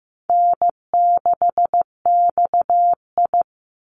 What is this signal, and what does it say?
Morse code, heard as a keyed tone, about 15 wpm, 700 Hz. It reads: N6XI
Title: Easy Call Signs Repeated 3x - 15wpm